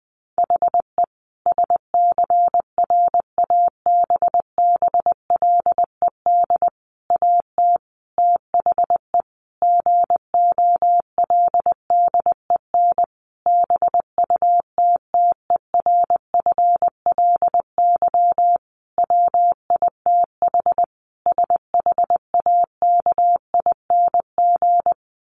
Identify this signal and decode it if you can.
Morse code, heard as a keyed tone, about 20 wpm, 700 Hz. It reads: HE SCRABBLED AT THE GOLDEN BUTTERFLY WITH SHAKING
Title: A Sound Of Thunder 20wpm